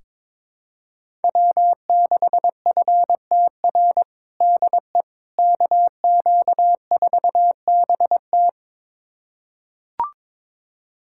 Morse code, heard as a keyed tone, about 22 wpm, 700 Hz. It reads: W6FTR DE KQ4BT
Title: USA Callsign DE Callsign - 5 Characters - 22wpm